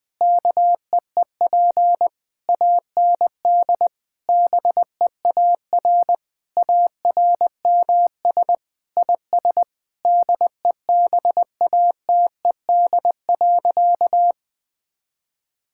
Morse code, heard as a keyed tone, about 20 wpm, 700 Hz. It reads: KEEP AND BEAR ARMS IS DEBATED.
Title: Sentences with 7-Word Phrases - 20wpm